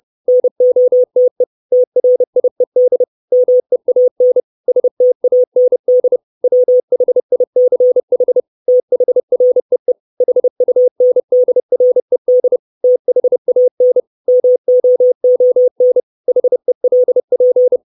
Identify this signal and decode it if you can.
Morse code, heard as a keyed tone, about 30 wpm, 500 Hz. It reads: NOTE TRIED MEAN STAND WHICH THREE HUNDRED THAN MOON HELP
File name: 632 words / 632 Words 30WPM